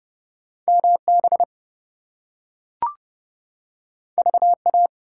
Morse code, heard as a keyed tone, about 32 wpm, 700 Hz. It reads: MB VA
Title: US States and Canadian Provinces 30wpm